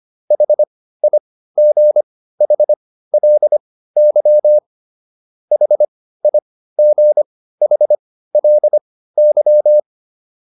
Morse code, heard as a keyed tone, about 25 wpm, spaced as for 15 wpm, 600 Hz. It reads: HIGHLY HIGHLY